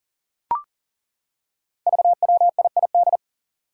Morse code, heard as a keyed tone, about 40 wpm, 700 Hz. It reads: VWIID
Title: Five Challenging Letter-Number Combinations 40wpm